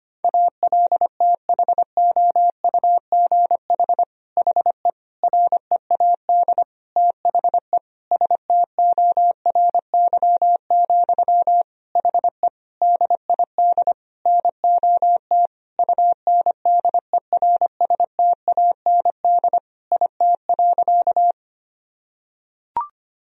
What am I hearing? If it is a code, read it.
Morse code, heard as a keyed tone, about 25 wpm, 700 Hz. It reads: ALTHOUGH HE READ THE STORY, HE DID NOT UNDERSTAND IT.
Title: Sentences from Top 1000 Words - Review Part 3 - 25wpm